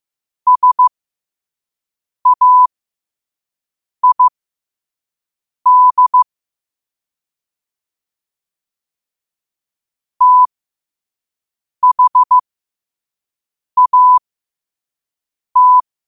Morse code, heard as a keyed tone, about 15 wpm, spaced as for 5 wpm, 1000 Hz. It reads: SAID THAT
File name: qotdfiles / 5wpm